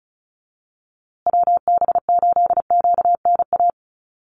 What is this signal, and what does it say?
Morse code, heard as a keyed tone, about 34 wpm, 700 Hz. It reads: WB8QNA
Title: USA Call Signs Repeated 3x 35wpm